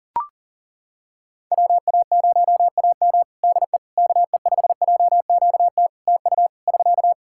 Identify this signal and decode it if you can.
Morse code, heard as a keyed tone, about 40 wpm, 700 Hz. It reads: WA0AM DE KE5JQT TU <SK>
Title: Practice QSO - Exchange 4 - Ending - 40wpm